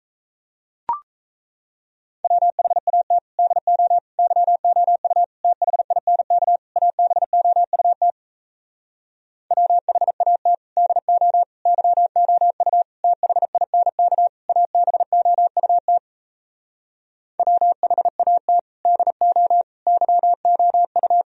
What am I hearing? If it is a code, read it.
Morse code, heard as a keyed tone, about 38 wpm, 700 Hz. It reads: WHAT DO YOU THINK ABOUT WHAT DO YOU THINK ABOUT WHAT DO YOU